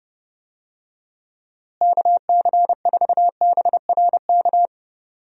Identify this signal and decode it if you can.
Morse code, heard as a keyed tone, about 30 wpm, 700 Hz. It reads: KC4BRK
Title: USA Call Signs - 6 Characters - Repeated 2x - 30wpm